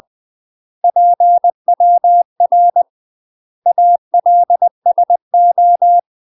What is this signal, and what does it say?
Morse code, heard as a keyed tone, about 20 wpm, 700 Hz. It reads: PWR ALSO
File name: UK MASTER QSOs TRAINING/20WPM